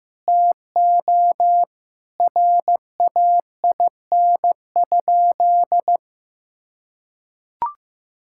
Morse code, heard as a keyed tone, about 15 wpm, 700 Hz. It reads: TO RAIN?
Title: Sentences from Top 1000 Words - Review Part 5 - 15wpm